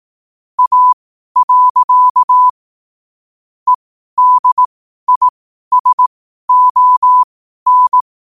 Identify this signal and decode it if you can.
Morse code, heard as a keyed tone, about 18 wpm, spaced as for 12 wpm, 1000 Hz. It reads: A. EDISON
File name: qotdfiles / 12wpm